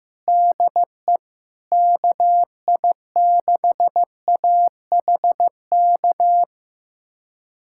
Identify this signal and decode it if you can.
Morse code, heard as a keyed tone, about 16 wpm, 700 Hz. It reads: DE KI6AHK
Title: USA Callsign DE Callsign - 6 Characters - 15wpm